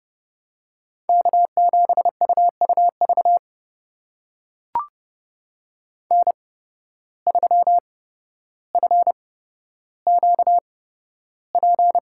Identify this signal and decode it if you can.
Morse code, heard as a keyed tone, about 30 wpm, spaced as for 13 wpm, 700 Hz. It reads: K7UUV N3FQP
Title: USA Call Signs - 5 Characters - 8x Character Spacing - 30wpm